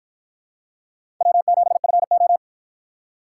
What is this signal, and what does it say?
Morse code, heard as a keyed tone, about 53 wpm, 700 Hz. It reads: W7FO